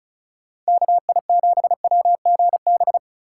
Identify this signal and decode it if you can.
Morse code, heard as a keyed tone, about 44 wpm, 700 Hz. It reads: KI7WGB